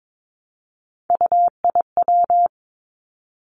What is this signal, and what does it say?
Morse code, heard as a keyed tone, about 22 wpm, 700 Hz. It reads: UIW
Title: Three Challenging Letter-Number Combinations 22wpm